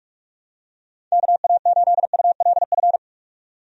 Morse code, heard as a keyed tone, about 43 wpm, 700 Hz. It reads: KA8ULF
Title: USA Call Signs - 6 Characters - 45wpm